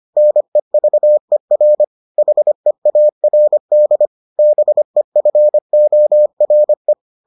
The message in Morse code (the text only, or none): NEVER HEARD BEFORE